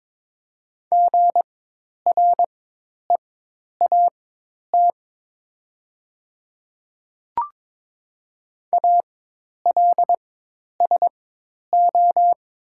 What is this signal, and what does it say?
Morse code, heard as a keyed tone, about 22 wpm, spaced as for 10 wpm, 700 Hz. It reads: GREAT ALSO
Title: 4x Character Spacing - Top 200 Words - 22wpm